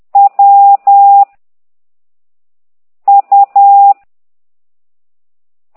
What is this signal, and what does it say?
Morse code, heard as a keyed tone, about 10 wpm, 800 Hz. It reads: W U
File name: ct2idw / exerc9